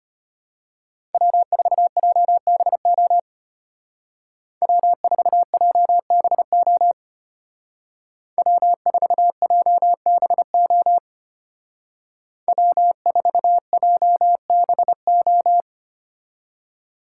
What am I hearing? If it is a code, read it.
Morse code, heard as a keyed tone, about 33 wpm, 700 Hz. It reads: W4JBO W4JBO W4JBO W4JBO